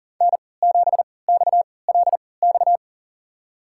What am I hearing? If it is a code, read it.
Morse code, heard as a keyed tone, about 37 wpm, spaced as for 22 wpm, 700 Hz. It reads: N7XLX
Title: USA Call Signs - 5 Characters - 3x Character Spacing - 40wpm